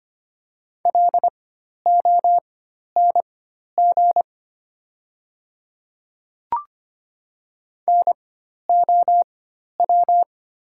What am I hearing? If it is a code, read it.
Morse code, heard as a keyed tone, about 25 wpm, spaced as for 12 wpm, 700 Hz. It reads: LONG NOW
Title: 4x Character Spacing - Top 100 Words - 25wpm